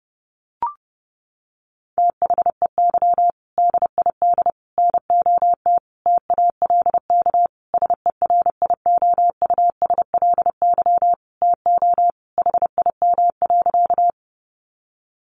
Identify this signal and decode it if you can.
Morse code, heard as a keyed tone, about 30 wpm, 700 Hz. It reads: THEY DID NOT TALK SERIOUSLY TO HIM.
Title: Sentences from Top 1200 Words 30wpm